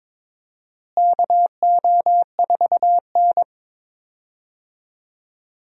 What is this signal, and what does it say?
Morse code, heard as a keyed tone, about 22 wpm, 700 Hz. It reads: KO4N
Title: USA Call Signs - 4 Characters - Repeated 3x - 22wpm